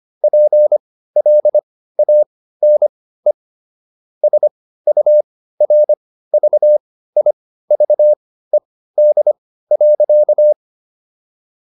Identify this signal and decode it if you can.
Morse code, heard as a keyed tone, about 25 wpm, spaced as for 15 wpm, 600 Hz. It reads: PLANE SURVIVED.